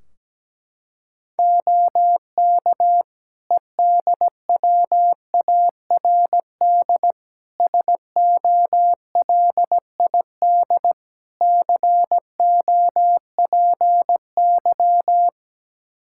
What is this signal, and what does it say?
Morse code, heard as a keyed tone, about 17 wpm, 700 Hz. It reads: OK EDWARD SOLID COPY
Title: Practice QSO - Exchange 3 - More Info - 17wpm